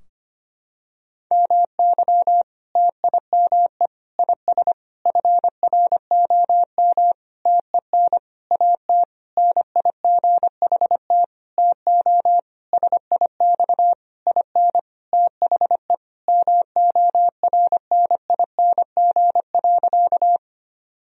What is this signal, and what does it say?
Morse code, heard as a keyed tone, about 25 wpm, 700 Hz. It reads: MY TIME IS FROM TEN AT NIGHT TO SIX IN THE MORNING.
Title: Sentences from Top 1000 Words - Review Part 2 - 25wpm